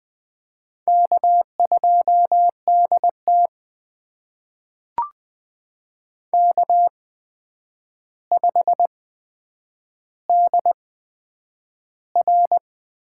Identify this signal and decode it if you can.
Morse code, heard as a keyed tone, about 20 wpm, spaced as for 9 wpm, 700 Hz. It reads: K2DT K5DR
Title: USA Call Signs - 4 Characters - 8x Character Spacing - 20wpm